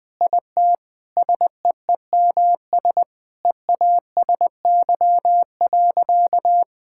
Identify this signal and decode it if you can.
Morse code, heard as a keyed tone, about 21 wpm, 700 Hz. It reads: IT SEEMS EASY.
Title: Sentences with 4-Word Phrases - Spoken Beginning - 20wpm